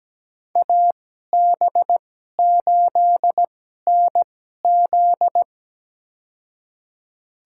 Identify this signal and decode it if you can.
Morse code, heard as a keyed tone, about 17 wpm, spaced as for 12 wpm, 700 Hz. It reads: AB8NZ
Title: USA Call Signs - 5 Characters - 2x Character Spacing - 17wpm